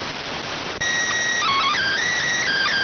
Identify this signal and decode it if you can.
SSTV (slow-scan television), PD 290.